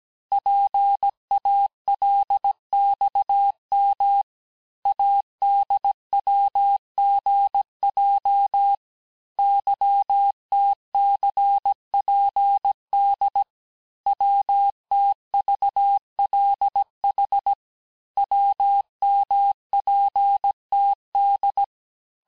Morse code, heard as a keyed tone, about 17 wpm, 800 Hz. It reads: PALXM ADWGJ YTCPD WTVLH WMPTD
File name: cours-du-lundi files / cours-1587-texte-chiffre-vitesse-16MPM